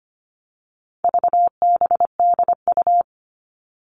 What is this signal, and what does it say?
Morse code, heard as a keyed tone, about 25 wpm, 700 Hz. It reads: VBDU
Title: All Cut Numbers - 4-Digit 25wpm